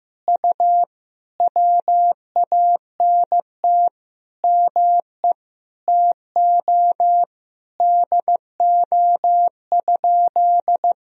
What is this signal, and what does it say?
Morse code, heard as a keyed tone, about 15 wpm, 700 Hz. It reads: U WANT ME TO DO?
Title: Sentences from Top 200 Words and Abbreviations 15wpm